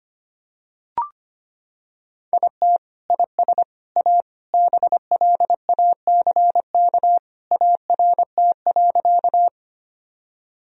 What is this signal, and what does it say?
Morse code, heard as a keyed tone, about 25 wpm, 700 Hz. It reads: IT IS A BLACK ART.
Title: Sentences from Top 1000 Words - Review Part 3 - 25wpm